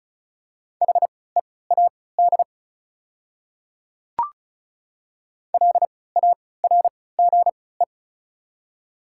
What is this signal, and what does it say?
Morse code, heard as a keyed tone, about 36 wpm, spaced as for 20 wpm, 700 Hz. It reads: HEAD LARGE
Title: 3x Character Spacing - Top 300 Words - 35wpm